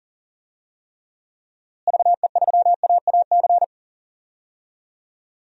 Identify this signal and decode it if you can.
Morse code, heard as a keyed tone, about 37 wpm, 700 Hz. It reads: VE3AAC
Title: DXCC-100 Call Signs - 6 Characters - Repeated 3x - 40wpm